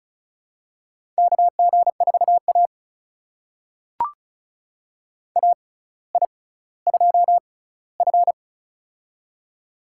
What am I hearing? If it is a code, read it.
Morse code, heard as a keyed tone, about 36 wpm, spaced as for 18 wpm, 700 Hz. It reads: KG4A AI2F
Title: USA Call Signs - 4 Characters - 6x Character Spacing - 35wpm